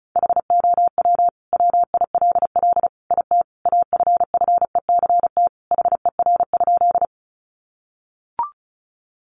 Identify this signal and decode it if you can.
Morse code, heard as a keyed tone, about 34 wpm, 700 Hz. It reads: HOW WILL IT AFFECT HER?
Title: Sentences from Top 900 Words 35wpm